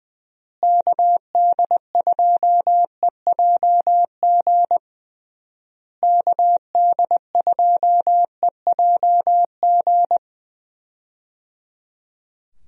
Morse code, heard as a keyed tone, about 20 wpm, 700 Hz. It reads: KD2EJG KD2EJG
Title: USA Call Signs - 6 Characters - Repeated 3x - 20wpm